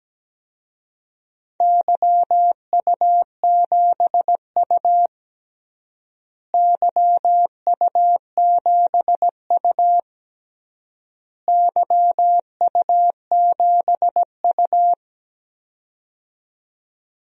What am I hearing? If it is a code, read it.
Morse code, heard as a keyed tone, about 17 wpm, 700 Hz. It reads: YU7U YU7U YU7U
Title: DXCC-100 Call Signs - 4 Characters - Repeated 3x - 17wpm